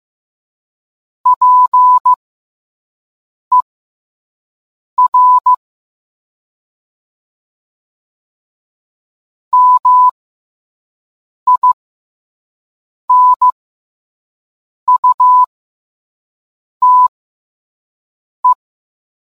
Morse code, heard as a keyed tone, about 15 wpm, spaced as for 5 wpm, 1000 Hz. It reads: PER MINUTE